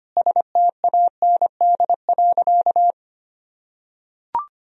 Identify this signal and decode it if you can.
Morse code, heard as a keyed tone, about 25 wpm, 700 Hz. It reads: STAND.
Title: Sentences from Top 1000 Words - Review Part 2 - 25wpm